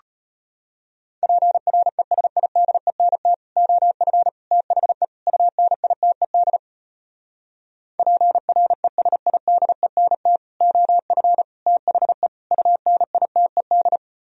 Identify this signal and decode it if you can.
Morse code, heard as a keyed tone, about 36 wpm, 700 Hz. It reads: PRESIDENT OF THE UNITED PRESIDENT OF THE UNITED